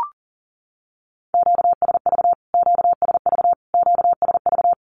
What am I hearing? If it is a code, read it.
Morse code, heard as a keyed tone, about 40 wpm, 700 Hz. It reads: QSV QSV QSV
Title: Q-Codes Repeated 3x 40wpm